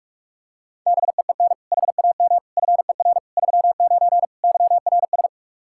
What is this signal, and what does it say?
Morse code, heard as a keyed tone, about 45 wpm, 700 Hz. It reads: BEEN HAM FER 39 YRS
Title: Practice QSO - Exchange 3 - More Info - 45wpm